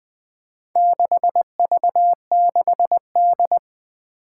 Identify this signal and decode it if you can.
Morse code, heard as a keyed tone, about 20 wpm, 700 Hz. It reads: 6V6D
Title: Four Challenging Letter-Number Combinations 20wpm